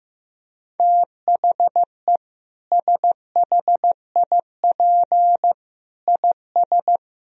Morse code, heard as a keyed tone, about 16 wpm, 700 Hz. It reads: THE SHIP IS
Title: Sentences from Top 1000 Words - Review Part 4 - 15wpm